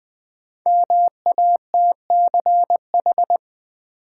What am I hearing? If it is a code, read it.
Morse code, heard as a keyed tone, about 21 wpm, 700 Hz. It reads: MATCH
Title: Sentences from Top 1600 Words - New Word Drill - 20wpm